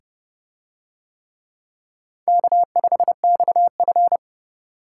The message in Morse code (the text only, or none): K5XF